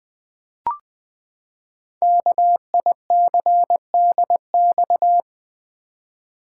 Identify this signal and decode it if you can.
Morse code, heard as a keyed tone, about 20 wpm, 700 Hz. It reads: KICDX
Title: Code Groups - 5 Letter:Numbers - 20wpm